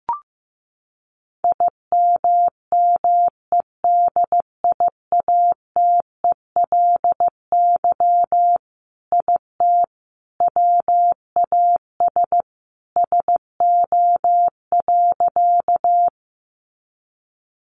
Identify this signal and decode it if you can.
Morse code, heard as a keyed tone, about 15 wpm, 700 Hz. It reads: IMMEDIATELY IT WAS SO.
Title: Sentences from Top 1100 Words 15wpm